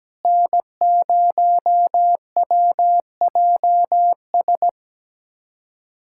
Morse code, heard as a keyed tone, about 17 wpm, 700 Hz. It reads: N0WJS
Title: USA Callsign DE Callsign - 5 Characters - 17wpm